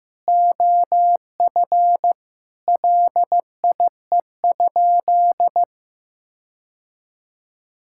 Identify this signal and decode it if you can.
Morse code, heard as a keyed tone, about 15 wpm, 700 Hz. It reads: OF LIE?
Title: Sentences from Top 1600 Words 15wpm